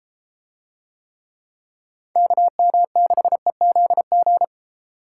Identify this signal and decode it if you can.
Morse code, heard as a keyed tone, about 33 wpm, 700 Hz. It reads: KM6EZG